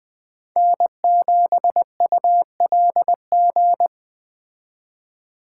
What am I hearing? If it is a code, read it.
Morse code, heard as a keyed tone, about 20 wpm, 700 Hz. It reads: N7ULG